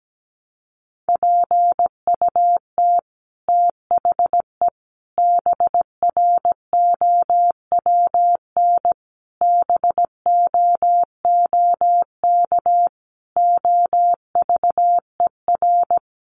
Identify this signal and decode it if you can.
Morse code, heard as a keyed tone, about 17 wpm, 700 Hz. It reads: PUT THE BROWN BOOK OVER
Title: Sentences from Top 1200 Words 17wpm